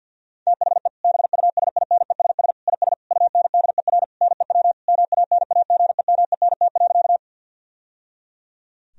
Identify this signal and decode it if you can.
Morse code, heard as a keyed tone, about 50 wpm, 700 Hz. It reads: THE BUSINESS IS UNDER NEW MANAGEMENT.
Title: Sentences from Top 1000 Words - Review Part 4 - 50wpm